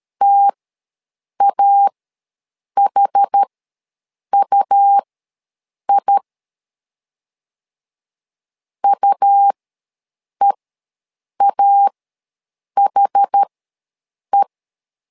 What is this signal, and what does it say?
Morse code, heard as a keyed tone, about 13 wpm, spaced as for 7 wpm, 800 Hz. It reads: TAHUI UEAHE
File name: LEZIONI / 2 UHDN / 2 UH ws 12wpm 1730